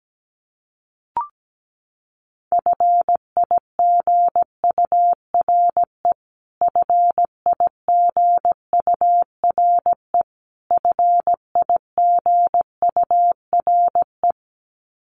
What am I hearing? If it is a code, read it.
Morse code, heard as a keyed tone, about 17 wpm, 700 Hz. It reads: FIGURE FIGURE FIGURE
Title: Top 500 Words Repeated 17wpm